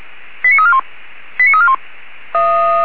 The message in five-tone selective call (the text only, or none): 89341 89341